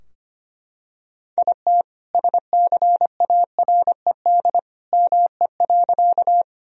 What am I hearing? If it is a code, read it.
Morse code, heard as a keyed tone, about 25 wpm, 700 Hz. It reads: IT SCARED ME.
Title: Sentences from Top 1000 Words - Review Part 4 - 25wpm